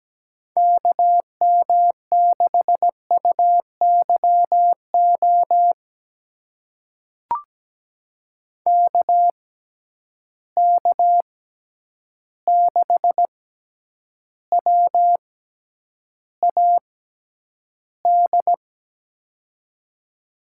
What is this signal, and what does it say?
Morse code, heard as a keyed tone, about 17 wpm, spaced as for 9 wpm, 700 Hz. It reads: KM6UYO KK6WAD
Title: USA Call Signs - 6 Characters - 6x Character Spacing - 17wpm